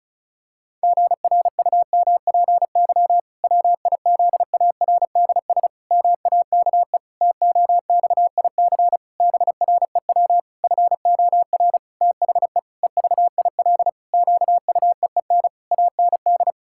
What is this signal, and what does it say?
Morse code, heard as a keyed tone, about 34 wpm, 700 Hz. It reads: GRUMPY WIZARDS MAKE TOXIC BREW FOR THE EVIL QUEEN AND
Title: Pangram Challenge v2 - 35wpm